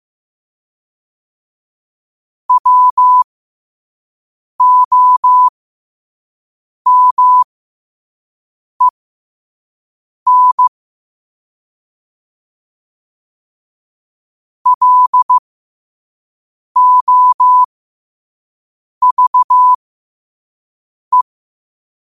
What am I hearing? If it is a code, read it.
Morse code, heard as a keyed tone, about 15 wpm, spaced as for 5 wpm, 1000 Hz. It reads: WOMEN LOVE
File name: qotdfiles / 5wpm